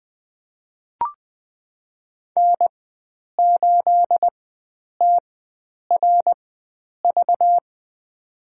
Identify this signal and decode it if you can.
Morse code, heard as a keyed tone, about 20 wpm, spaced as for 9 wpm, 700 Hz. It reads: N8TRV
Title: USA Call Signs - 5 Characters - 4x Character Spacing - 20wpm